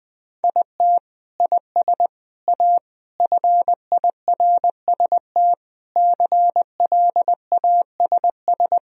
Morse code, heard as a keyed tone, about 20 wpm, 700 Hz. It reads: IT IS A FIRST CLASS
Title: Sentences from Top 1000 Words - Review Part 2 - 20wpm